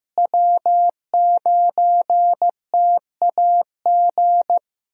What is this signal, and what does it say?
Morse code, heard as a keyed tone, about 15 wpm, 700 Hz. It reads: W9TAG
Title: USA Call Signs - 5 Characters - Repeated 3x - 15wpm